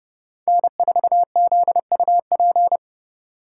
Morse code, heard as a keyed tone, about 37 wpm, 700 Hz. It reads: N4ZUP
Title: Speed Racing - USA 1x3 Call Signs 30wpm